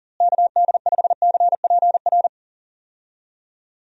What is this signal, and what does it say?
Morse code, heard as a keyed tone, about 40 wpm, 700 Hz. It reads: KD5CPR
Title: USA Call Signs - 6 Characters - Repeated 2x - 40wpm